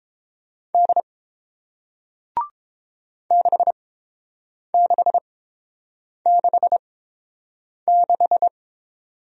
Morse code, heard as a keyed tone, about 30 wpm, spaced as for 7 wpm, 700 Hz. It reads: D 6666